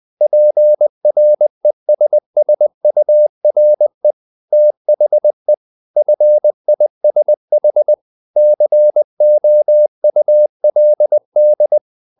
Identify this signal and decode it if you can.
Morse code, heard as a keyed tone, about 20 wpm, 600 Hz. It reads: PRESSURE THE FISH COULD
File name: the old man and the sea / the old man and the sea part 1 20wpm